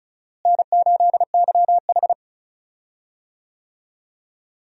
Morse code, heard as a keyed tone, about 36 wpm, 700 Hz. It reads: N8YH
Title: USA Call Signs - 4 Characters - Repeated 2x - 35wpm